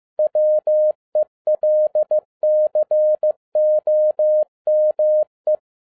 Morse code, heard as a keyed tone, about 15 wpm, 610 Hz. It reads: WELCOME
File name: 2012-01-mp3 / n17-SUMMARY OF FEA NET IN 2011-15wpm